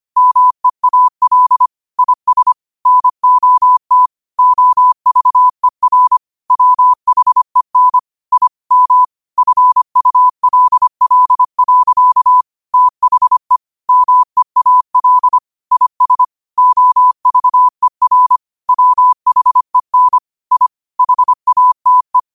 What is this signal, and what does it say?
Morse code, heard as a keyed tone, about 25 wpm, 1000 Hz. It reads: MEAL IS NOT OVER WHEN I M FULL. THE MEAL IS OVER WHEN I HATE